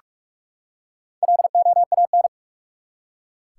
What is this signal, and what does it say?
Morse code, heard as a keyed tone, about 46 wpm, 700 Hz. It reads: LOAN
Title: Four Letter Words - 45wpm